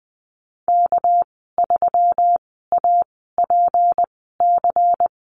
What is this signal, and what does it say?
Morse code, heard as a keyed tone, about 20 wpm, spaced as for 15 wpm, 700 Hz. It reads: K3APC
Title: 219 Lesson 35g - Number 0 - Call Signs 20wpm